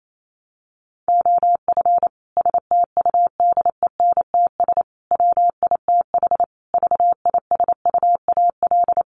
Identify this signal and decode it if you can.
Morse code, heard as a keyed tone, about 28 wpm, 700 Hz. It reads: OF STUDENTS WITH VISUAL
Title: Sets of 5 Words Encore 28wpm